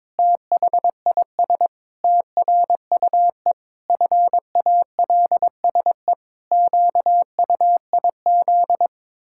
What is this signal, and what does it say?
Morse code, heard as a keyed tone, about 22 wpm, 700 Hz. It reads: THIS TRUE FALSE QUIZ
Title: Pangram Challenge v2 - 22wpm